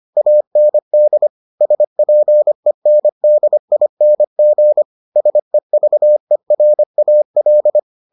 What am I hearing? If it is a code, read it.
Morse code, heard as a keyed tone, about 25 wpm, 600 Hz. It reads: AND SPENDING SEVERAL